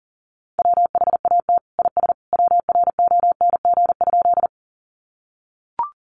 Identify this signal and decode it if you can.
Morse code, heard as a keyed tone, about 40 wpm, 700 Hz. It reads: WHAT IS WRONG?
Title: Sentences from Top 900 Words 40wpm